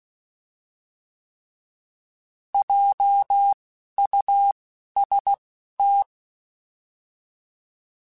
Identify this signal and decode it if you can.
Morse code, heard as a keyed tone, about 16 wpm, spaced as for 11 wpm, 800 Hz. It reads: JUST
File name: Alphabetical / Alphabetical mp3s / Js @10